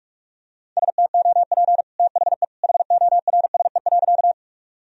Morse code, heard as a keyed tone, about 46 wpm, 700 Hz. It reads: STOP THE HORSE.
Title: Sentences from Top 1000 Words - Review Part 1 - 45wpm